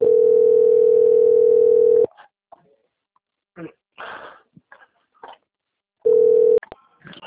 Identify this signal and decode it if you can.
Ringback tone (North America), 440 + 480 Hz.